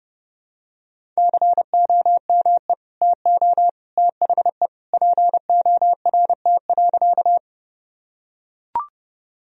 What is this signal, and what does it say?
Morse code, heard as a keyed tone, about 32 wpm, 700 Hz. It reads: COME TO THE PORT.
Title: Sentences from Top 1000 Words - Review Part 1 - 30wpm